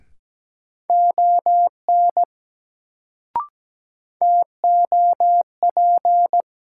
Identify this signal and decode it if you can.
Morse code, heard as a keyed tone, about 17 wpm, 700 Hz. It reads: ON TOP
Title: Sets of 2 Words - New Word Drill - 17wpm